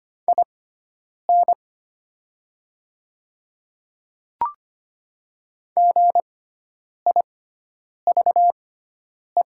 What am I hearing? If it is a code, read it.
Morse code, heard as a keyed tone, about 25 wpm, spaced as for 9 wpm, 700 Hz. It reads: IN GIVE
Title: 6x Character Spacing - Top 200 Words - 25wpm